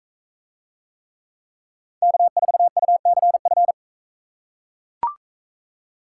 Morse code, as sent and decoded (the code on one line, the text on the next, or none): -.- ....- ..- -.-. ..-.
K4UCF